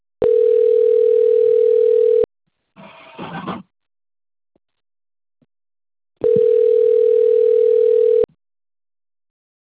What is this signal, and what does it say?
Ringback tone (North America), 440 + 480 Hz.
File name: rg-1600-unknown- - - .74180